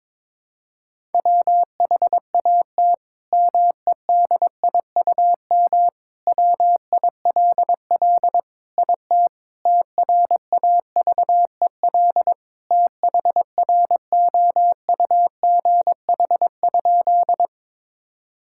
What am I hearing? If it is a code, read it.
Morse code, heard as a keyed tone, about 22 wpm, 700 Hz. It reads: WHAT MEDIUM WILL IT TRAVEL THROUGH?
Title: Sentences from Top 1000 Words - Review Part 6 - 22wpm